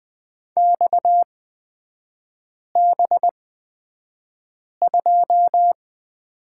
Morse code, heard as a keyed tone, about 20 wpm, spaced as for 5 wpm, 690 Hz. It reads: XB2
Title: 229 Lesson 37e - Letter x - Three Characters 20wpm